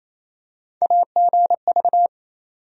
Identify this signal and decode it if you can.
Morse code, heard as a keyed tone, about 28 wpm, 700 Hz. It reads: AGV
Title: USA Call Sign Postfixes - 3 Letters 28wpm